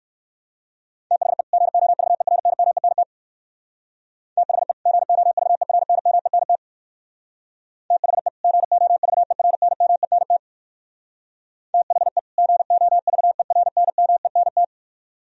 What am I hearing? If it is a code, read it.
Morse code, heard as a keyed tone, about 58 wpm, 700 Hz. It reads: THE GOVERNMENT THE GOVERNMENT THE GOVERNMENT THE GOVERNMENT